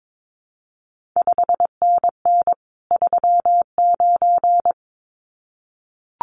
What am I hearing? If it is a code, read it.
Morse code, heard as a keyed tone, about 22 wpm, 700 Hz. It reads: 5NN 39
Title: CQ WW DX Contest Exchange - 22wpm